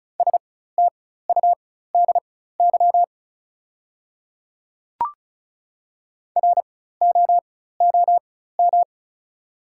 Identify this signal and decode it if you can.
Morse code, heard as a keyed tone, about 34 wpm, spaced as for 16 wpm, 690 Hz. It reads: STUDY ROOM
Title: 4x Character Spacing - Top 300 Words - 35wpm